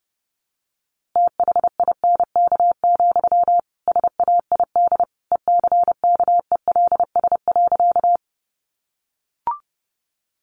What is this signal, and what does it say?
Morse code, heard as a keyed tone, about 30 wpm, 700 Hz. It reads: THINK, SAID ECKELS.
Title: A Sound Of Thunder 30wpm